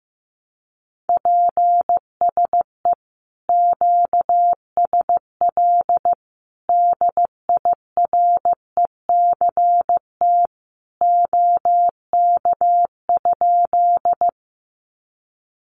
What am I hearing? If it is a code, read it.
Morse code, heard as a keyed tone, about 15 wpm, 700 Hz. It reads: PSE QSL DIRECT OK?
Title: Common QSO Phrases 15wpm